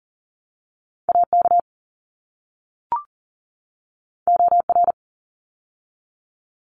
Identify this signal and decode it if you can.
Morse code, heard as a keyed tone, about 40 wpm, 700 Hz. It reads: AK OR
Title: ARRL Sections 40wpm